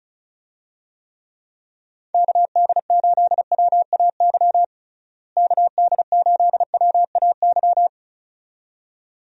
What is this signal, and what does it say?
Morse code, heard as a keyed tone, about 34 wpm, 700 Hz. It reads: KD8WAY KD8WAY
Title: USA Call Signs - 6 Characters - Repeated 2x - 35wpm